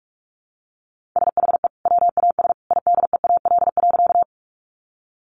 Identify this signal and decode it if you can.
Morse code, heard as a keyed tone, about 44 wpm, 700 Hz. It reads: SHE WAS IDEAL.
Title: Sentences from Top 1400 Words 45wpm